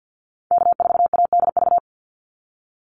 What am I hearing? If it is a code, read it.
Morse code, heard as a keyed tone, about 50 wpm, 700 Hz. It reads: K4ADV
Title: NA SOTA Activator Callsigns 50wpm